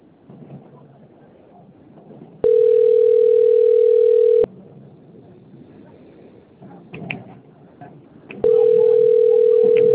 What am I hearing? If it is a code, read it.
Ringback tone (North America), 440 + 480 Hz.